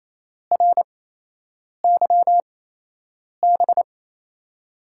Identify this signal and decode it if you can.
Morse code, heard as a keyed tone, about 28 wpm, spaced as for 8 wpm, 700 Hz. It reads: RYB
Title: USA Call Signs - 6 Characters - 8x Character Spacing - 28wpm